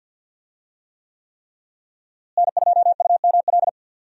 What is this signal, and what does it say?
Morse code, heard as a keyed tone, about 49 wpm, 700 Hz. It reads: N2UML
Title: USA Call Signs - 5 Characters - Repeated 3x - 50wpm